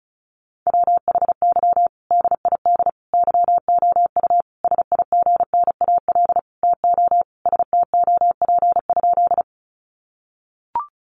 Morse code, heard as a keyed tone, about 34 wpm, 700 Hz. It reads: WHY DID YOU SIGNAL TO STOP?
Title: Sentences from Top 1400 Words 35wpm